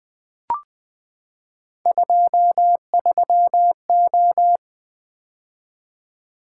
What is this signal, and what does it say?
Morse code, heard as a keyed tone, about 20 wpm, 700 Hz. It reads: 23O
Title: Three Letter-Number Combinations - 20wpm